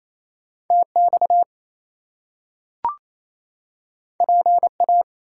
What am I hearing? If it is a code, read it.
Morse code, heard as a keyed tone, about 28 wpm, 700 Hz. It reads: TX PA
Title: US States and Canadian Provinces 28wpm